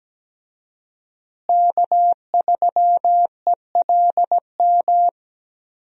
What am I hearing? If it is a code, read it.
Morse code, heard as a keyed tone, about 17 wpm, 700 Hz. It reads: K3ELM
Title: USA Call Signs - 5 Characters - Repeated 2x - 17wpm